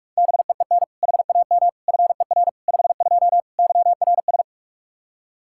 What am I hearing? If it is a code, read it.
Morse code, heard as a keyed tone, about 46 wpm, 700 Hz. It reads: BEEN HAM FER 52 YRS
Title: Practice QSO - Exchange 3 - More Info - 45wpm